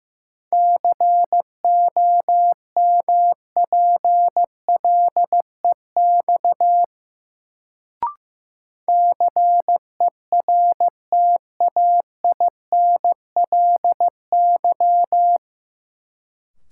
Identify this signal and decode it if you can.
Morse code, heard as a keyed tone, about 15 wpm, 700 Hz. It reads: COMPLEX CERTAINLY
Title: Sentences from Top 900 Words - New Word Drill - 15wpm